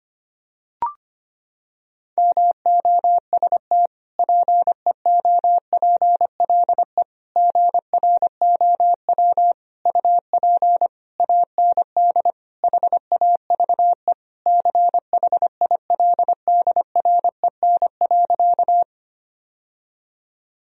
Morse code, heard as a keyed tone, about 25 wpm, 700 Hz. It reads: MOST PEOPLE GROW UP AND HAVE CHILDREN.
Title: Sentences from Top 1000 Words - Review Part 1 - 25wpm